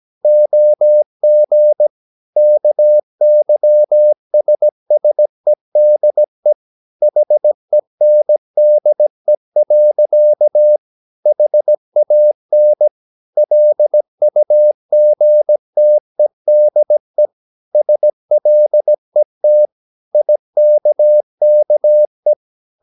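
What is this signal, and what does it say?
Morse code, heard as a keyed tone, about 17 wpm, 600 Hz. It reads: OG KYSSEDE HENDE. HAN LUGTEDE SLET IKKE
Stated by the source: Slow (QRS) Morse practice at 17 wpm, from CW med Gnister.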